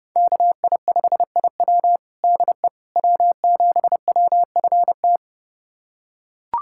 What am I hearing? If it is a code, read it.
Morse code, heard as a keyed tone, about 30 wpm, 700 Hz. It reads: KI5IW DE W7WFT
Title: USA Callsign DE Callsign 30wpm